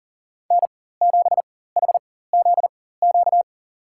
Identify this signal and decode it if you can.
Morse code, heard as a keyed tone, about 39 wpm, spaced as for 19 wpm, 700 Hz. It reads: N7HZQ
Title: USA Call Signs - 4x Character Spacing - 40wpm